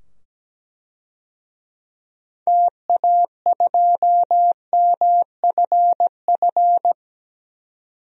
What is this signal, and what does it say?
Morse code, heard as a keyed tone, about 17 wpm, 700 Hz. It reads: TA2MFF
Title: DXCC-100 Call Signs - 6 Characters - 17wpm